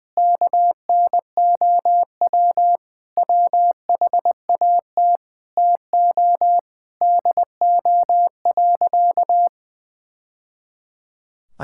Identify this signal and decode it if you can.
Morse code, heard as a keyed tone, about 19 wpm, 700 Hz. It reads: KNOW WHAT TO DO.
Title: Sentences from Top 1000 Words - Review Part 5 - 20wpm